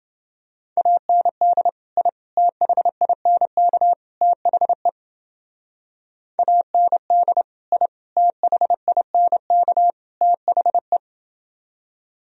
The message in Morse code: AND I THINK THE AND I THINK THE

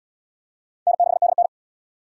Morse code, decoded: N6DN

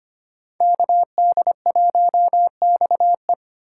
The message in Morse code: KD1XE

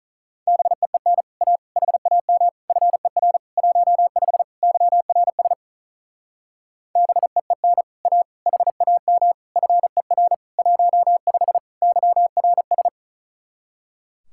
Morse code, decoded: BEEN A HAM FER 15 YRS BEEN A HAM FER 15 YRS

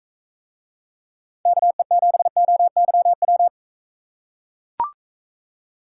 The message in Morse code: KE7OYW

